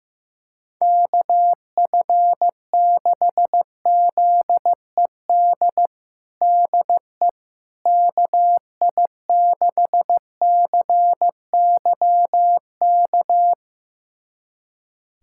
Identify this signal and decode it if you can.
Morse code, heard as a keyed tone, about 15 wpm, 700 Hz. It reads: KF6ZED DE KI6CYK
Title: USA Callsign DE Callsign - 6 Characters - 15wpm